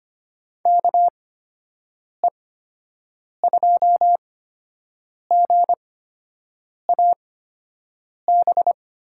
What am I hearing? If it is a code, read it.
Morse code, heard as a keyed tone, about 25 wpm, spaced as for 7 wpm, 700 Hz. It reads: KE2GAB
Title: USA Call Signs - 6 Characters - 8x Character Spacing - 25wpm